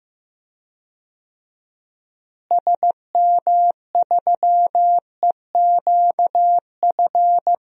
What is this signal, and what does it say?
Morse code, heard as a keyed tone, about 15 wpm, 700 Hz. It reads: SM3EQF
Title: DXCC-100 Call Signs - 6 Characters - Repeated 3x - 15wpm